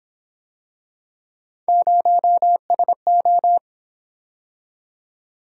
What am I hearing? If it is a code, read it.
Morse code, heard as a keyed tone, about 27 wpm, 700 Hz. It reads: 0SO
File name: Speed-Racing - Three Letter-Number Combinations - 22wpm